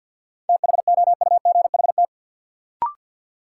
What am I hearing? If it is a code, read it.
Morse code, heard as a keyed tone, about 48 wpm, 700 Hz. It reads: THOUGHT
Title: Sets of 2 Words - New Word Drill - 50wpm